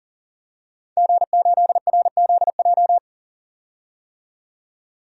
Morse code, heard as a keyed tone, about 40 wpm, 700 Hz. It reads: G8RZJ
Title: DXCC-100 Call Signs - 5 Characters - 40wpm